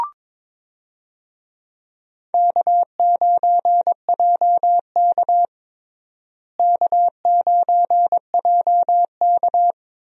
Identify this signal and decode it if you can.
Morse code, heard as a keyed tone, about 22 wpm, 700 Hz. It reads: K9JK K9JK
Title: USA Call Signs - 4 Characters - Repeated 2x - 22wpm